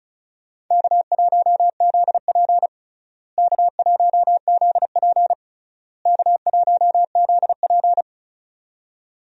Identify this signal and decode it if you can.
Morse code, heard as a keyed tone, about 34 wpm, 700 Hz. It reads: K1ZP K1ZP K1ZP
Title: USA Call Signs - 4 Characters - Repeated 3x - 35wpm